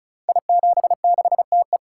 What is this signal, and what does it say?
Morse code, heard as a keyed tone, about 36 wpm, 700 Hz. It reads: I76TE
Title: Code Groups - 5 Letter:Numbers - 35wpm